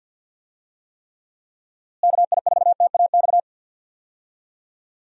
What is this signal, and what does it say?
Morse code, heard as a keyed tone, about 45 wpm, 700 Hz. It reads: KI4TAX
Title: USA Call Signs - 6 Characters - Repeated 3x - 50wpm